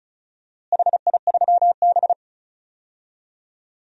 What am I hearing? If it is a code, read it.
Morse code, heard as a keyed tone, about 34 wpm, 700 Hz. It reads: HI3B
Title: DXCC-100 Call Signs - 4 Characters - Repeated 2x - 35wpm